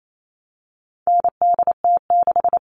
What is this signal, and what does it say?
Morse code, heard as a keyed tone, about 28 wpm, 700 Hz. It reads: NDT6
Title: All Cut Numbers - 4-Digit 28wpm